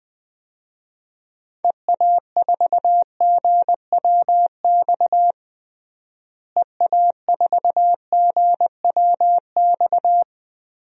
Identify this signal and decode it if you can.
Morse code, heard as a keyed tone, about 20 wpm, 700 Hz. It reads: EA4GWX EA4GWX
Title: DXCC-100 Call Signs - 6 Characters - Repeated 2x - 20wpm